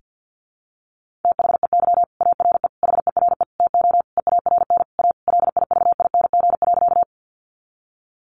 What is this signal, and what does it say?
Morse code, heard as a keyed tone, about 50 wpm, 700 Hz. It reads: THEY ARE HERE TO EARN A LIVING.
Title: Sentences from Top 1500 Words 50wpm